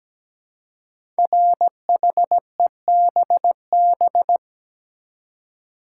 Morse code, heard as a keyed tone, about 17 wpm, 700 Hz. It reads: RHEBB
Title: Code Groups - 5 Letters - 17wpm